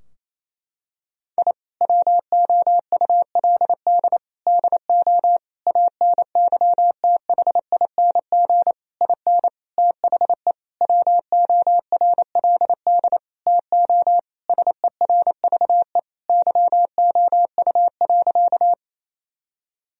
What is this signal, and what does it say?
Morse code, heard as a keyed tone, about 28 wpm, 700 Hz. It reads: I WOULD DO ANYTHING IN THE WORLD TO SERVE YOU.
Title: Sentences from Top 1000 Words - Review Part 2 - 28wpm